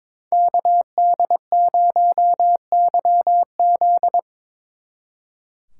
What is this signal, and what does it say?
Morse code, heard as a keyed tone, about 26 wpm, 700 Hz. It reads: KD0YZ
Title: Speed Racing - USA 2x2 Call Signs 22wpm